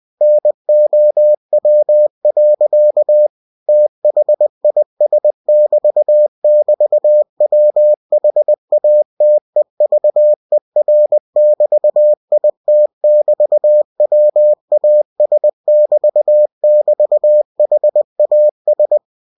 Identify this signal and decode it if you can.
Morse code, heard as a keyed tone, about 20 wpm, 600 Hz. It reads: NOW. THIS<BT><BT>WHATEVER<BT>IT<BT>WAS<BT><BT>HAS
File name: cw-books / winnie-the-pooh / winnie-20wpm-page-0022